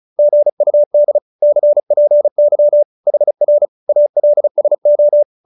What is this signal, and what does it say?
Morse code, heard as a keyed tone, about 34 wpm, 570 Hz. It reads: GUD CPY HR ALSO
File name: QSO314 35